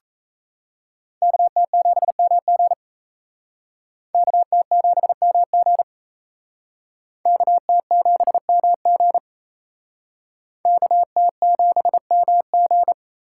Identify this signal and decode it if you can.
Morse code, heard as a keyed tone, about 36 wpm, 700 Hz. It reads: KT7MG KT7MG KT7MG KT7MG